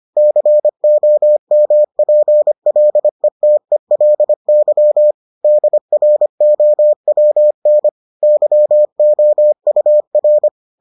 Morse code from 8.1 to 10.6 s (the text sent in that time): YOUR